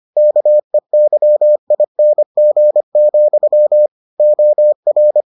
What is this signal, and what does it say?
Morse code, heard as a keyed tone, about 25 wpm, 600 Hz. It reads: KEYING, OR